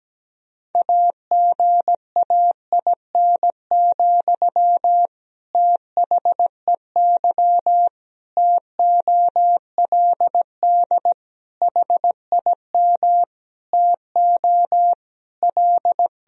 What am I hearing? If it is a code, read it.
Morse code, heard as a keyed tone, about 17 wpm, 700 Hz. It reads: AGAIN, THEY TOLD HIM TO L
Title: Sentences from Top 1000 Words - Review Part 3 - 17wpm